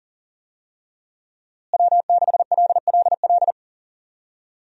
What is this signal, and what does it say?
Morse code, heard as a keyed tone, about 40 wpm, 690 Hz. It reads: W6LLL
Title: USA Call Signs - 5 Characters - Repeated 3x - 40wpm